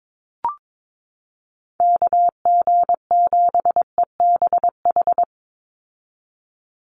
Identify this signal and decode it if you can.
Morse code, heard as a keyed tone, about 22 wpm, 700 Hz. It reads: KG7EBH
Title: USA Call Signs 22wpm